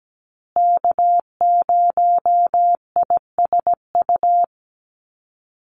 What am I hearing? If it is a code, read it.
Morse code, heard as a keyed tone, about 17 wpm, 700 Hz. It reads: K0ISU
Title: USA Call Signs Repeated 2x 17wpm